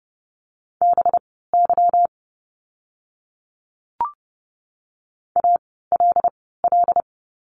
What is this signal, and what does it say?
Morse code, heard as a keyed tone, about 30 wpm, spaced as for 17 wpm, 700 Hz. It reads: BY ALL
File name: course / 30:10wpm / 237 Lesson 38f - Letter q - Words 30wpm